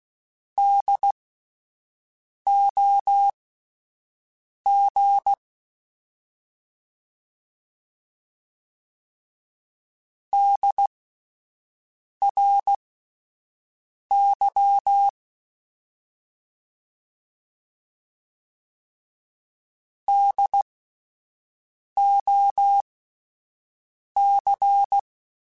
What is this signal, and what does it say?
Morse code, heard as a keyed tone, about 16 wpm, spaced as for 5 wpm, 800 Hz. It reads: DOG DRY DOC